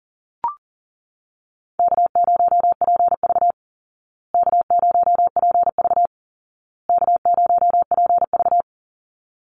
Morse code, heard as a keyed tone, about 40 wpm, 700 Hz. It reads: K0PV K0PV K0PV
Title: Easy Call Signs Repeated 3x - 40wpm